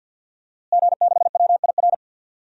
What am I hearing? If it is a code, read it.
Morse code, heard as a keyed tone, about 46 wpm, 700 Hz. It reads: G6WIR
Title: DXCC-100 Call Signs - 5 Characters - Repeated 2x - 50wpm